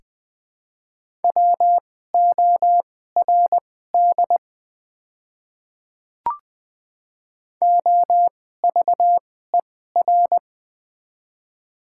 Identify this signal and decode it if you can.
Morse code, heard as a keyed tone, about 20 wpm, spaced as for 14 wpm, 700 Hz. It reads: WORD OVER
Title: 2x Character Spacing - Top 100 Words - 20wpm